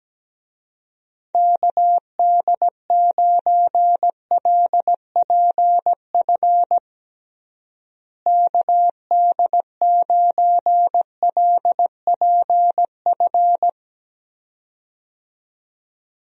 Morse code, heard as a keyed tone, about 17 wpm, 700 Hz. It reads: KD9LPF KD9LPF
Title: USA Call Signs - 6 Characters - Repeated 2x - 17wpm